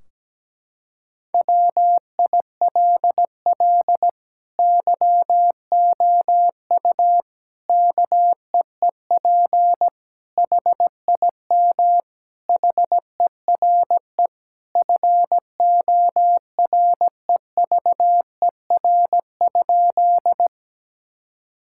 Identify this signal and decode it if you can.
Morse code, heard as a keyed tone, about 17 wpm, 700 Hz. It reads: WILL YOU KEEP HIM HERE FOREVER?
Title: Sentences from Top 1600 Words 17wpm